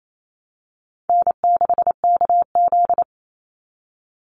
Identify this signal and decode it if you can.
Morse code, heard as a keyed tone, about 28 wpm, 700 Hz. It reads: N6KZ
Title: NA SOTA Chaser Callsigns 28wpm